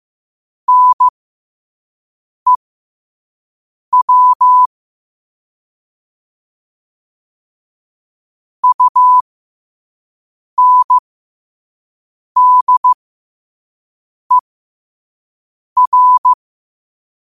Morse code, heard as a keyed tone, about 15 wpm, spaced as for 5 wpm, 1000 Hz. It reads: NEW UNDER
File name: qotdfiles / 5wpm